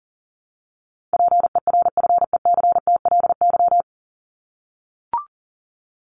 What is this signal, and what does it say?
Morse code, heard as a keyed tone, about 40 wpm, 700 Hz. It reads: PERFECTLY
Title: Top 1501-1750 Words 40wpm